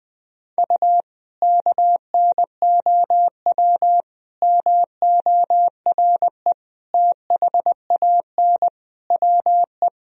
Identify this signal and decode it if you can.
Morse code, heard as a keyed tone, about 21 wpm, 700 Hz. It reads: U KNOW MORE THAN WE
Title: Sentences from Top 100 Words and Abbreviations 20wpm